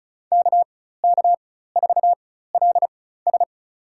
Morse code, heard as a keyed tone, about 34 wpm, spaced as for 16 wpm, 700 Hz. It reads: KK4LS
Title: USA Call Signs - 5 Characters - 4x Character Spacing - 35wpm